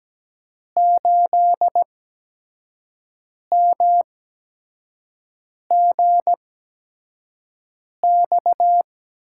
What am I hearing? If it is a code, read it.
Morse code, heard as a keyed tone, about 17 wpm, spaced as for 5 wpm, 700 Hz. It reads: 8MGX
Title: USA Call Signs - 8x Character Spacing - 17wpm